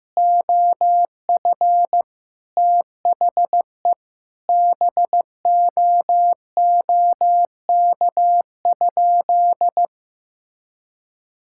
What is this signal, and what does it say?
Morse code, heard as a keyed tone, about 15 wpm, 700 Hz. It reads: OF THE BOOK?
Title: Sentences from Top 1000 Words - Review Part 5 - 15wpm